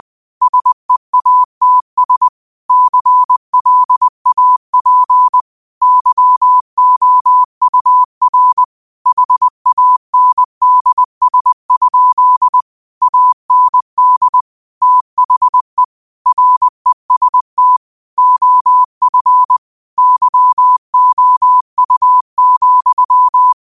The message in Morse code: SEATS CLAP YOUR HANDS? AND THE REST OF YOU,

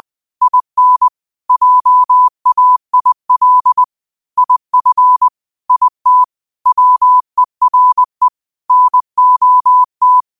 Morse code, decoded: IN JAIL IF IT WERE NOT

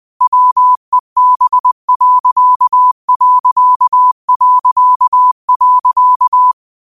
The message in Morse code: WEB....

